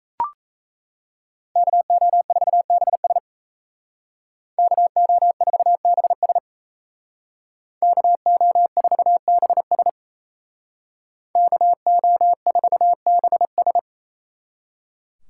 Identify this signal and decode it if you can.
Morse code, heard as a keyed tone, about 34 wpm, 700 Hz. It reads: KO4BS KO4BS KO4BS KO4BS
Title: Speed Racing - USA 2x2 Call Signs 28wpm